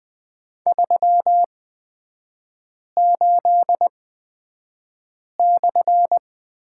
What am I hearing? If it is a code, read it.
Morse code, heard as a keyed tone, about 20 wpm, spaced as for 5 wpm, 700 Hz. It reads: 38/
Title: 236 Lesson 38e - Letter q - Three Characters 20wpm